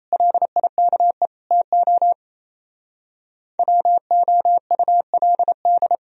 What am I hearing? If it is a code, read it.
Morse code, heard as a keyed tone, about 34 wpm, 700 Hz. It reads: LIKE TO WOULD
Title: Speed-Racing - Sets of 3 Words 28wpm